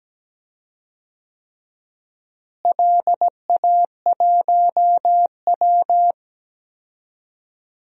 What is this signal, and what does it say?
Morse code, heard as a keyed tone, about 17 wpm, 700 Hz. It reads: LA1W
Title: DXCC-100 Call Signs - 4 Characters - Repeated 3x - 17wpm